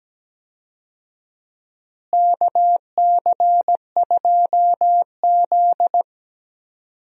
Morse code, heard as a keyed tone, about 17 wpm, 700 Hz. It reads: KC2Z
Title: USA Call Signs - 4 Characters - Repeated 2x - 17wpm